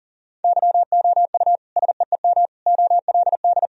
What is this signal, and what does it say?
Morse code, heard as a keyed tone, about 40 wpm, 700 Hz. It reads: YOU SEEM OLD
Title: Sentences from Top 1000 Words - Review Part 1 - 40wpm